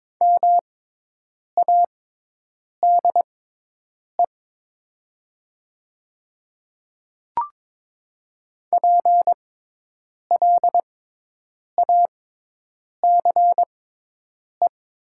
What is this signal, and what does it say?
Morse code, heard as a keyed tone, about 22 wpm, spaced as for 8 wpm, 700 Hz. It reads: MADE PLACE
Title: 6x Character Spacing - Top 200 Words - 22wpm